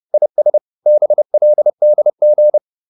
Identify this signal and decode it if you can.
Morse code, heard as a keyed tone, about 30 wpm, 600 Hz. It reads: IS BLDG